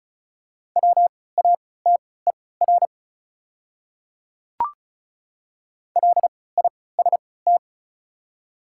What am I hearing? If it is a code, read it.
Morse code, heard as a keyed tone, about 36 wpm, spaced as for 20 wpm, 690 Hz. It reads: WATER LIST
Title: 3x Character Spacing - Top 500 Words - 35wpm